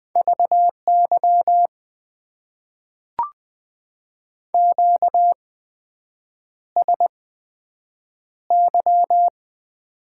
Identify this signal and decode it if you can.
Morse code, heard as a keyed tone, about 20 wpm, spaced as for 7 wpm, 700 Hz. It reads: VY QSY
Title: Top 90 QSO Elements - 8x Character Spacing - 20wpm